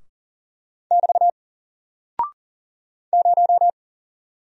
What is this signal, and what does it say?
Morse code, heard as a keyed tone, about 39 wpm, 700 Hz. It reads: <BT> 0
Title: Single Character (Letter:Number:Punctuation:Prosign) 40wpm